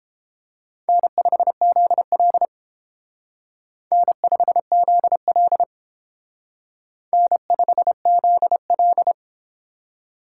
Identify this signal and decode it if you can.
Morse code, heard as a keyed tone, about 28 wpm, 700 Hz. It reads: N5ZL N5ZL N5ZL